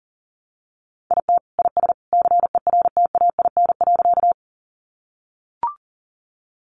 Morse code, heard as a keyed tone, about 40 wpm, 700 Hz. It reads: IT IS CERTAIN.
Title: Sentences from Top 500 Words 40wpm